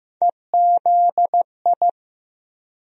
Morse code, heard as a keyed tone, about 15 wpm, 700 Hz. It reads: EZI
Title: USA Call Sign Postfixes - 3 Letters 15wpm